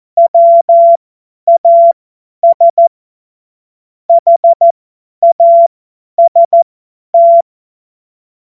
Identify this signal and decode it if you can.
Morse code, heard as a keyed tone, about 14 wpm, spaced as for 10 wpm, 680 Hz. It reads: WAS HAST
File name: Morsekurs / Witze / Witz 03. August 2021, 50 F70